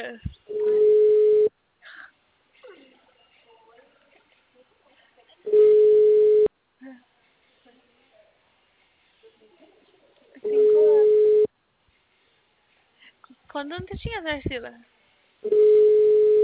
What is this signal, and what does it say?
Special information tone, with speech.
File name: out- -2002